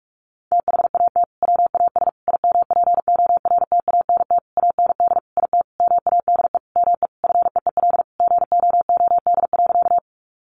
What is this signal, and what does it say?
Morse code, heard as a keyed tone, about 44 wpm, 700 Hz. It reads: THAT WAS IMPORTANT AND IT MADE ME FEEL GOOD.
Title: Sentences from Top 600 Words 45wpm